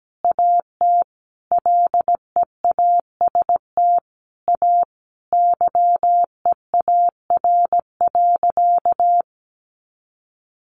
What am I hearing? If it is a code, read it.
Morse code, heard as a keyed tone, about 17 wpm, 700 Hz. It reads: AT LEAST A YEAR.
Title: Sentences from Top 700 Words 17wpm